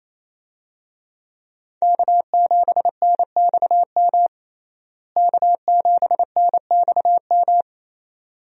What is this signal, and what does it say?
Morse code, heard as a keyed tone, about 27 wpm, 700 Hz. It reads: K7NXM K7NXM
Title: USA Call Signs - 5 Characters - Repeated 2x - 28wpm